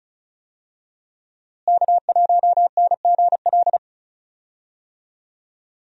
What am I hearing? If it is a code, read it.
Morse code, heard as a keyed tone, about 36 wpm, 700 Hz. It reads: K1NGL
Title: USA Call Signs - 5 Characters - Repeated 3x - 35wpm